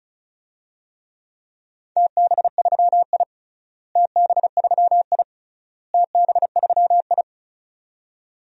Morse code, heard as a keyed tone, about 36 wpm, 700 Hz. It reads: TB3I TB3I TB3I
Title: DXCC-100 Call Signs - 4 Characters - Repeated 3x - 35wpm